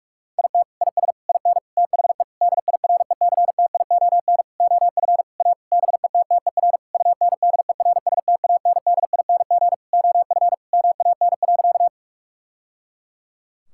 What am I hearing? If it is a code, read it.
Morse code, heard as a keyed tone, about 41 wpm, 700 Hz. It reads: IT IS IN THE DIRECTION OF A BETTER UNDERSTANDING OF MAN.
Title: Sentences from Top 1000 Words - Review Part 5 - 45wpm